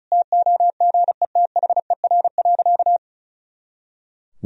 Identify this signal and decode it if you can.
Morse code, heard as a keyed tone, about 36 wpm, 700 Hz. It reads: TOGETHER.
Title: Sentences from Top 400 Words and Abbreviations 35wpm